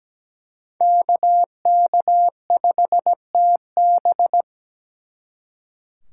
Morse code, heard as a keyed tone, about 20 wpm, 700 Hz. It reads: KK5TB
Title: Speed Racing - USA Call Signs 17wpm